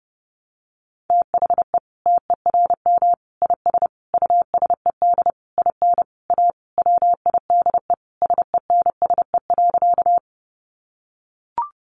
Morse code, heard as a keyed tone, about 30 wpm, 700 Hz. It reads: THE TERM IS USED IN A WIDE SENSE.
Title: Sentences from Top 900 Words 30wpm